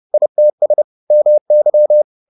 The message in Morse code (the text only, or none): ITS MY